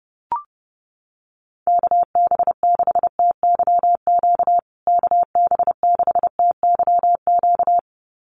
Morse code, heard as a keyed tone, about 30 wpm, 700 Hz. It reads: KB6TYQ KB6TYQ
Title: USA Call Signs Repeated 2x 30wpm